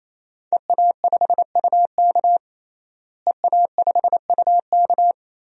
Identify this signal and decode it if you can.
Morse code, heard as a keyed tone, about 28 wpm, 700 Hz. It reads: EA5UK EA5UK
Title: DXCC-100 Call Signs - 5 Characters - Repeated 3x - 28wpm